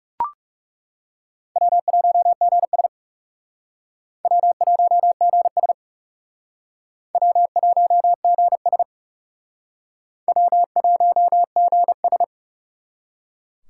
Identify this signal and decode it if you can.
Morse code, heard as a keyed tone, about 37 wpm, 700 Hz. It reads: W1GS W1GS W1GS W1GS
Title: Speed Racing - USA 1x2 Call Signs 30wpm